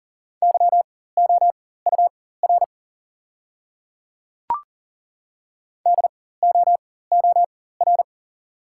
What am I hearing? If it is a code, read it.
Morse code, heard as a keyed tone, about 40 wpm, spaced as for 19 wpm, 690 Hz. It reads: YOUR DOOR
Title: 4x Character Spacing - Top 300 Words - 40wpm